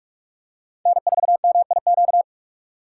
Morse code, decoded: N4MIQ

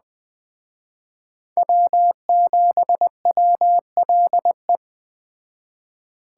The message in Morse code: W7WLE

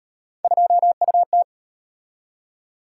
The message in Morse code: 2UT

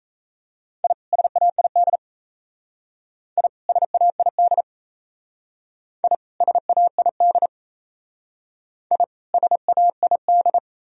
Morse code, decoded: I SAID I SAID I SAID I SAID

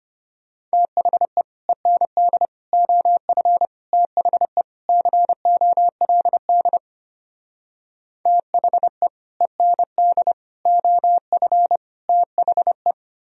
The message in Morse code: THE END OF THE COLD THE END OF THE

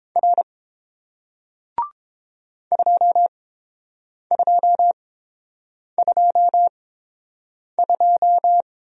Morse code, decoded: R 2222

